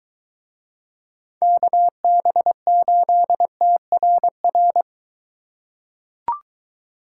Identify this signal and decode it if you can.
Morse code, heard as a keyed tone, about 23 wpm, 700 Hz. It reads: KB8TRR